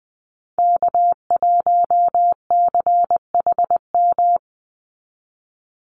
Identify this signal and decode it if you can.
Morse code, heard as a keyed tone, about 20 wpm, 700 Hz. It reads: K1CHM
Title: Call Signs Repeated 3x - 20wpm